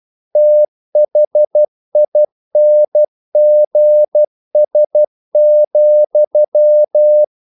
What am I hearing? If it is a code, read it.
Morse code, heard as a keyed tone, about 12 wpm, 600 Hz. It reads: THINGS,
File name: CW / Practice / EnglishQuotations0312